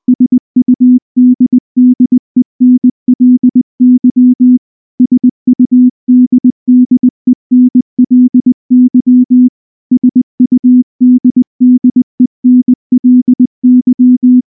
Morse code, heard as a keyed tone, about 19 wpm, 250 Hz. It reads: SUDDENLY SUDDENLY SUDDENLY
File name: morse / morse-training-20WPM-10